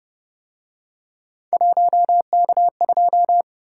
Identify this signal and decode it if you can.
Morse code, heard as a keyed tone, about 29 wpm, 700 Hz. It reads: 1K2